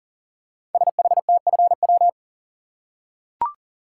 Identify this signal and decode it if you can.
Morse code, heard as a keyed tone, about 38 wpm, 700 Hz. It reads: SHTFW
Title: Code Groups - 5 Letters - 40wpm